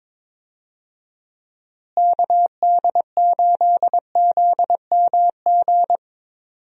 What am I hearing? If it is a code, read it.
Morse code, heard as a keyed tone, about 22 wpm, 700 Hz. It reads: KD8ZMG
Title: USA Call Signs - 6 Characters - Repeated 2x - 22wpm